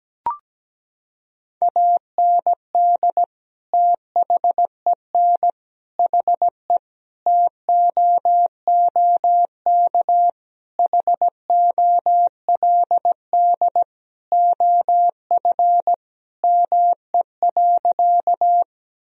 Morse code, heard as a keyed tone, about 17 wpm, 700 Hz. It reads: AND THEN HE TOOK HOLD OF ME.
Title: Sentences from Top 1000 Words - Review Part 2 - 17wpm